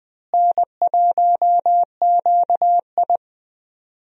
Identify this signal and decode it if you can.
Morse code, heard as a keyed tone, about 21 wpm, 700 Hz. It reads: N1QI
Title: USA Call Signs - 4 Characters - Repeated 3x - 20wpm